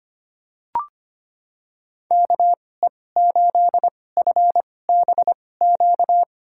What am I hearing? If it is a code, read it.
Morse code, heard as a keyed tone, about 25 wpm, spaced as for 18 wpm, 700 Hz. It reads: KE8FBQ
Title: USA Call Signs - 6 Characters - 2x Character Spacing - 25wpm